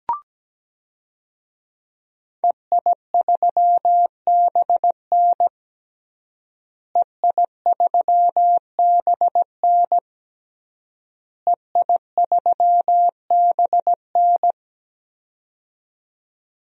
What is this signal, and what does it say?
Morse code, heard as a keyed tone, about 17 wpm, 700 Hz. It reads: EI3BN EI3BN EI3BN
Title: DXCC-100 Call Signs - 5 Characters - Repeated 3x - 17wpm